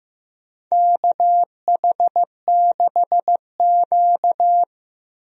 Morse code, heard as a keyed tone, about 15 wpm, 700 Hz. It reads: KH6Q
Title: DXCC-100 Call Signs - 4 Characters - Repeated 3x - 15wpm